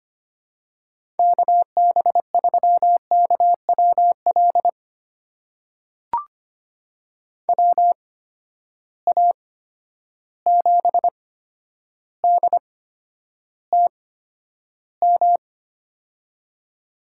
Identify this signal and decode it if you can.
Morse code, heard as a keyed tone, about 25 wpm, spaced as for 11 wpm, 700 Hz. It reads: KB3KWL WA7DTM
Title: USA Call Signs - 6 Characters - 8x Character Spacing - 25wpm